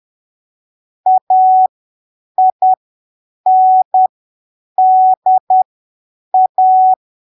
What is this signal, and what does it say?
Morse code, heard as a keyed tone, about 10 wpm, spaced as for 7 wpm, 750 Hz. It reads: AINDA
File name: arquivos / texto / texto 20 10ppm